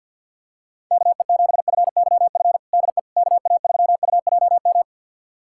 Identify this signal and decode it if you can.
Morse code, heard as a keyed tone, about 51 wpm, 700 Hz. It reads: KE7FYF DE KA3UJM
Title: USA Callsign DE Callsign - 6 Characters - 50wpm